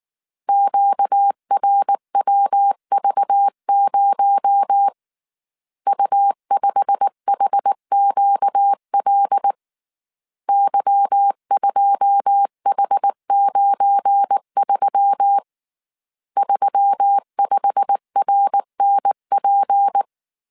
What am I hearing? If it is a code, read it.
Morse code, heard as a keyed tone, about 19 wpm, 800 Hz. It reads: QRWV0 U5HQL Y2H93 35RNP
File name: CW / DET / 1 9 1500C S A 18wpm 2459